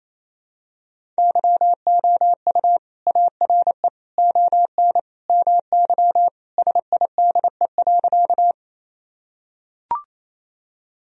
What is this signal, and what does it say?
Morse code, heard as a keyed tone, about 28 wpm, 700 Hz. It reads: YOU ARE ON MY SIDE.
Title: Sentences from Top 1000 Words - Review Part 1 - 28wpm